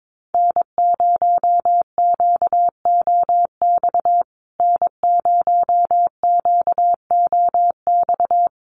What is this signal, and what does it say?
Morse code, heard as a keyed tone, about 22 wpm, 700 Hz. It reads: N0QOX N0QOX
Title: USA Call Signs Repeated 3x 22wpm